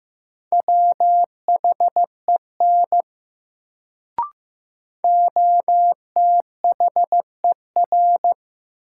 Morse code, heard as a keyed tone, about 15 wpm, 700 Hz. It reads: WHEN OTHER
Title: Sentences from Top 100 Words - New Word Drill - 15wpm